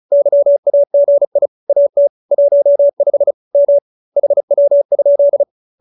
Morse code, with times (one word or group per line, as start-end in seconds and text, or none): 0.1-1.5 YAGI
1.7-2.1 AT
2.3-3.3 15
3.5-3.8 M
4.2-5.4 HW?